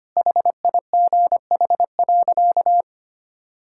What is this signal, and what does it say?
Morse code, heard as a keyed tone, about 25 wpm, 700 Hz. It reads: HIGH.
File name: practice / 25wpm / Sentences with 3-Word Phrases - 25wpm